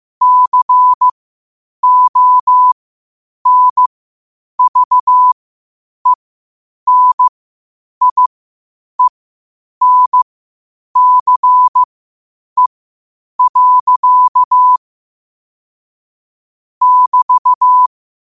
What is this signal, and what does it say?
Morse code, heard as a keyed tone, about 15 wpm, spaced as for 8 wpm, 1000 Hz. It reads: CONVENIENCE. <BT>
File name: qotdfiles / 8wpm